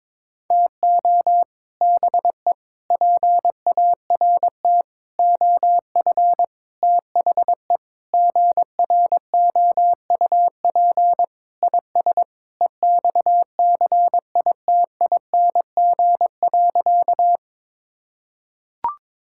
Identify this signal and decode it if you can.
Morse code, heard as a keyed tone, about 22 wpm, 700 Hz. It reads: TO BE PART OF THE GROUP IS EXCITING.
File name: practice / 22wpm / Sentences with 4-Word Phrases - 22wpm